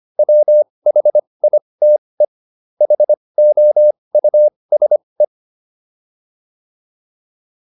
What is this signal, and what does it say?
Morse code, heard as a keyed tone, about 25 wpm, spaced as for 20 wpm, 600 Hz. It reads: WHITE HOUSE